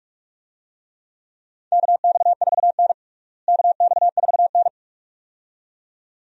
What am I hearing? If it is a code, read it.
Morse code, heard as a keyed tone, about 42 wpm, 700 Hz. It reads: KX4N KX4N
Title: USA Call Signs - 4 Characters - Repeated 2x - 45wpm